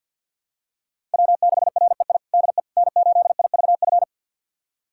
Morse code, heard as a keyed tone, about 46 wpm, 700 Hz. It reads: W6REI DE N8IVF
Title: USA Callsign DE Callsign 50wpm